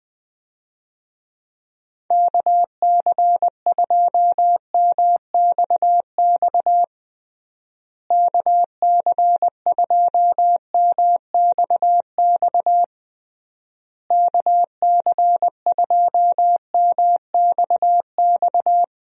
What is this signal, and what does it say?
Morse code, heard as a keyed tone, about 19 wpm, 700 Hz. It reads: KC2MXX KC2MXX KC2MXX
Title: USA Call Signs - 6 Characters - Repeated 3x - 20wpm